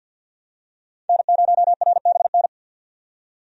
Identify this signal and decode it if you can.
Morse code, heard as a keyed tone, about 46 wpm, 700 Hz. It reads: N0RBN
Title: USA Call Signs - 5 Characters - 50wpm